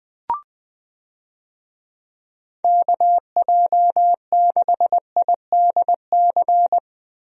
Morse code, heard as a keyed tone, about 21 wpm, 700 Hz. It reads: KJ6IDC
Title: USA Call Signs - 6 Characters - Repeated 2x - 20wpm